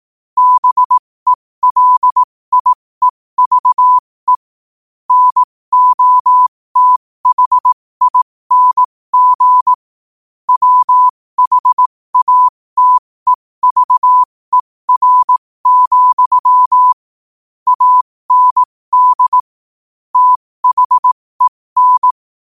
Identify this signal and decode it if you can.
Morse code, heard as a keyed tone, about 18 wpm, spaced as for 15 wpm, 1000 Hz. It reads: BELIEVE NOTHING WHATEVER, AND THEN
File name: qotdfiles / 15wpm